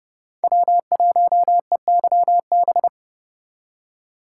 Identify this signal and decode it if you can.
Morse code, heard as a keyed tone, about 31 wpm, 700 Hz. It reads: W1EYB
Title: USA Call Signs - 4x Character Spacing - 30wpm